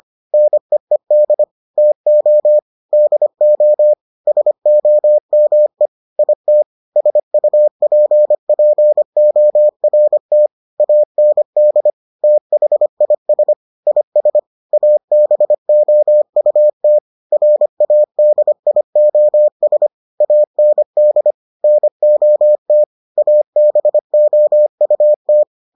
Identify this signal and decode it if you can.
Morse code, heard as a keyed tone, about 25 wpm, 600 Hz. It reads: NEED TO DO SOME IT SUPPORT AND THIS IS ABOUT RADIOS AND NOT ABOUT